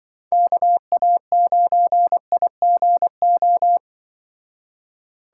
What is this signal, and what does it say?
Morse code, heard as a keyed tone, about 25 wpm, 700 Hz. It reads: KA9IGO
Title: Speed Racing - USA 2x3 Call Signs 20wpm